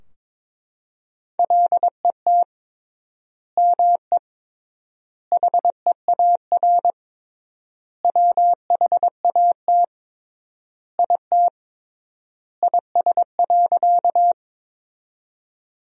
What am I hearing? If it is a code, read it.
Morse code, heard as a keyed tone, about 22 wpm, 700 Hz. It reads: LET ME HEAR WHAT IT IS.
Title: 3x Word-Spacing - Sentences from Top 300 Words - 22wpm